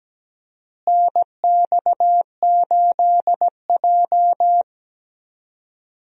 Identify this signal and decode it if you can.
Morse code, heard as a keyed tone, about 17 wpm, 700 Hz. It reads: NX8J
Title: USA Call Signs - 4 Characters - Repeated 3x - 17wpm